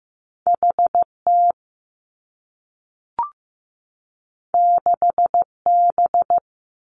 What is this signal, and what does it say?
Morse code, heard as a keyed tone, about 15 wpm, 700 Hz. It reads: H T 6 B
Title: Two Challenging Letter-Number Combinations 15wpm